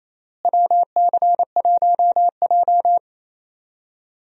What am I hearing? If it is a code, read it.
Morse code, heard as a keyed tone, about 28 wpm, 700 Hz. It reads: WC1J
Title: USA Call Signs - 4 Characters - Repeated 3x - 28wpm